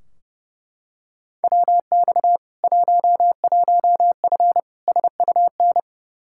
Morse code, heard as a keyed tone, about 30 wpm, 700 Hz. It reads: WX 11F SUN
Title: Practice QSO - Exchange 2 - Info - 30wpm